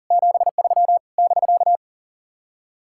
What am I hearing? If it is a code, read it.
Morse code, heard as a keyed tone, about 48 wpm, 700 Hz. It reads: 73 <BK>
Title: Speed-Racing Common QSO Phrases Encore - 40wpm